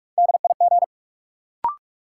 Morse code, heard as a keyed tone, about 42 wpm, 700 Hz. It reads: DIG
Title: Sentences from Top 1600 Words - New Word Drill - 45wpm